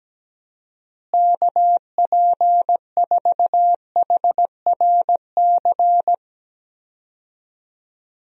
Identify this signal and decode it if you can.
Morse code, heard as a keyed tone, about 17 wpm, 700 Hz. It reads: KP4HRC
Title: USA Call Signs - 6 Characters - Repeated 2x - 17wpm